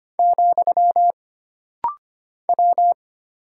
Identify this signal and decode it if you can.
Morse code, heard as a keyed tone, about 25 wpm, 700 Hz. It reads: , W
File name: Single Character (Letter:Number:Punctuation:Prosign) - Rapid-Fire - 25wpm